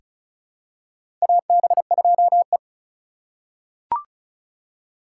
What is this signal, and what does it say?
Morse code, heard as a keyed tone, about 36 wpm, 700 Hz. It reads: AB2E
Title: USA Call Signs - 4 Characters - 3x Character Spacing - 35wpm